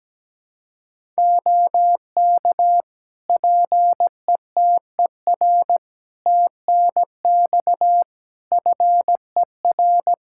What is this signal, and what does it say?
Morse code, heard as a keyed tone, about 17 wpm, 700 Hz. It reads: OK PETER TNX FER
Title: Practice QSO - Exchange 4 - Ending - 17wpm